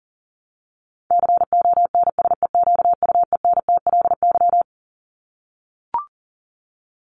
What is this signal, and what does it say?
Morse code, heard as a keyed tone, about 40 wpm, 700 Hz. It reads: CONSEQUENTLY
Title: LY Suffix - Top 100 Words with Base Words 40wpm